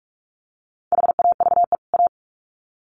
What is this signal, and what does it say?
Morse code, heard as a keyed tone, about 44 wpm, 700 Hz. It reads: HAVE A
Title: Sets of 2 Words 45wpm